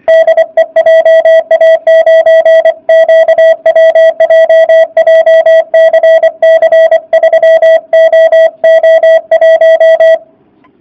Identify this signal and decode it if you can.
Morse code, heard as a keyed tone, about 24 wpm, 650 Hz. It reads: DE JA9QWJ JCC 3OO1